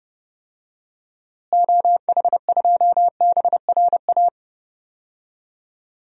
Morse code, heard as a keyed tone, about 30 wpm, 700 Hz. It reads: OH2BRA
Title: DXCC-100 Call Signs - 6 Characters - Repeated 2x - 30wpm